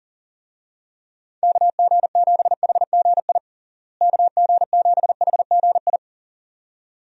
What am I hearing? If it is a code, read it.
Morse code, heard as a keyed tone, about 40 wpm, 700 Hz. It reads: KG7HGI KG7HGI
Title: USA Call Signs - 6 Characters - Repeated 2x - 40wpm